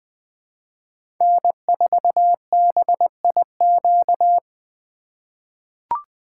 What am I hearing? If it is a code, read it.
Morse code, heard as a keyed tone, about 21 wpm, 700 Hz. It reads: N4BIQ
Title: USA Call Signs - 5 Characters - 2x Character Spacing - 20wpm